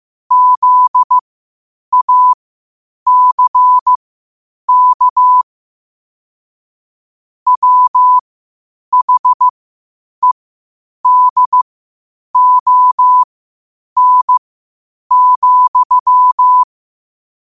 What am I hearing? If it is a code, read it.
Morse code, heard as a keyed tone, about 15 wpm, spaced as for 8 wpm, 1000 Hz. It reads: ZACK WHEDON,
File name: qotdfiles / 8wpm